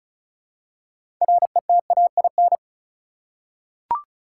Morse code, heard as a keyed tone, about 36 wpm, 700 Hz. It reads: RETAIN
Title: Six Letter Words - 35wpm